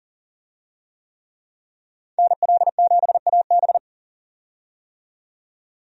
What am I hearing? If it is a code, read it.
Morse code, heard as a keyed tone, about 40 wpm, 700 Hz. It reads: NL7AB
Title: USA Call Signs - 5 Characters - 40wpm